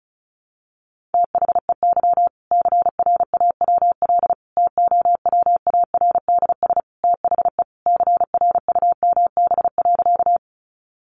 Morse code, heard as a keyed tone, about 34 wpm, 700 Hz. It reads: THEY CRAWL TOWARDS THE CRUMB.
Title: Easy Phrases 35wpm